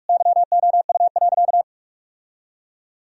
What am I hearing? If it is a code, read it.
Morse code, heard as a keyed tone, about 42 wpm, 700 Hz. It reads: YOU.
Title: Sentences from Top 1000 Words - Review Part 2 - 45wpm